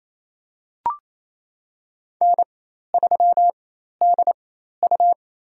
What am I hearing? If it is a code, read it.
Morse code, heard as a keyed tone, about 28 wpm, spaced as for 13 wpm, 700 Hz. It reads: N3DU
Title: USA Call Signs - 4x Character Spacing - 28wpm